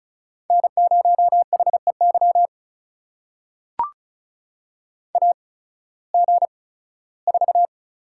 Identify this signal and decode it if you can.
Morse code, heard as a keyed tone, about 34 wpm, spaced as for 18 wpm, 700 Hz. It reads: N0HEY AG4
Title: USA Call Signs - 5 Characters - 8x Character Spacing - 35wpm